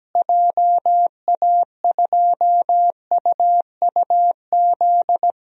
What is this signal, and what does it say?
Morse code, heard as a keyed tone, about 17 wpm, 700 Hz. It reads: JA2UUZ
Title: DXCC-100 Call Signs - 6 Characters - Repeated 3x - 17wpm